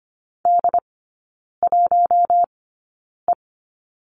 Morse code, heard as a keyed tone, about 25 wpm, spaced as for 9 wpm, 700 Hz. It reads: D1E
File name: Lesson 17e - Letter u - Three Characters 25wpm